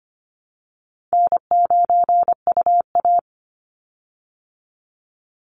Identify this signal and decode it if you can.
Morse code, heard as a keyed tone, about 25 wpm, 700 Hz. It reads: N9UA
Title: Call Signs Repeated 3x - 25wpm